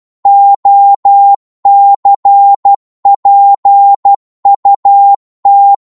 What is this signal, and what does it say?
Morse code, heard as a keyed tone, about 12 wpm, 800 Hz. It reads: OCPUT
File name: Morse / ltr-12wpm